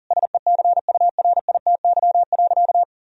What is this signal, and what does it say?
Morse code, heard as a keyed tone, about 40 wpm, 700 Hz. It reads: SECURITY.
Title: Sentences from Top 1000 Words - Review Part 4 - 40wpm